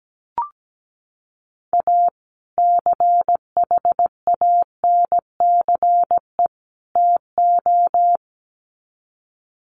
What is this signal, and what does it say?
Morse code, heard as a keyed tone, about 17 wpm, 700 Hz. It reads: A CHANCE TO
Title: Sets of 3 Words 17wpm